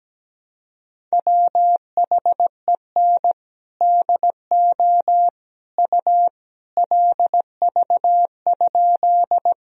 Morse code, heard as a keyed tone, about 17 wpm, 700 Hz. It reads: WHEN DO U LV?
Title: Sentences from Top 400 Words and Abbreviations 17wpm